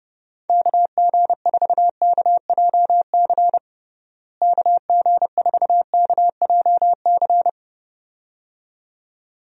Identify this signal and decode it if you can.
Morse code, heard as a keyed tone, about 32 wpm, 700 Hz. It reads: KG4KJC KG4KJC
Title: USA Call Signs - 6 Characters - Repeated 2x - 30wpm